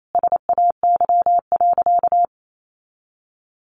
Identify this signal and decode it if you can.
Morse code, heard as a keyed tone, about 28 wpm, 700 Hz. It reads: SAY.
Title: Sentences from Top 1500 Words 28wpm